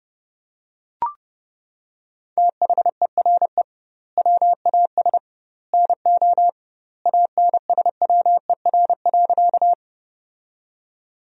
Morse code, heard as a keyed tone, about 32 wpm, 700 Hz. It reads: THERE WAS NO ANSWER.
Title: 2x Word-Spacing - Sentences from Top 300 Words - 30wpm